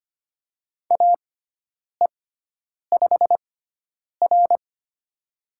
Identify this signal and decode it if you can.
Morse code, heard as a keyed tone, about 25 wpm, spaced as for 9 wpm, 700 Hz. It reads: AE5R
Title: USA Call Signs - 4 Characters - 6x Character Spacing - 25wpm